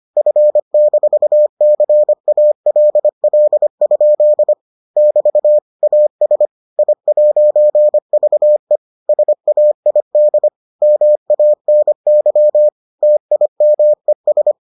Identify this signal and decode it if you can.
Morse code, heard as a keyed tone, about 25 wpm, 600 Hz. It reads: F-CALL? <BT> AS I'VE SAID MANY TIMES